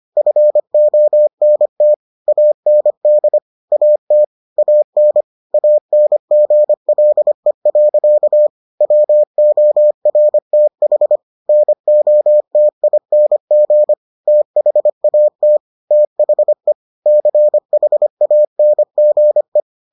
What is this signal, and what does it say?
Morse code, heard as a keyed tone, about 25 wpm, 600 Hz. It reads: FONT AND AT AN ANGLE. WORTH NOTING THAT THE CHANGE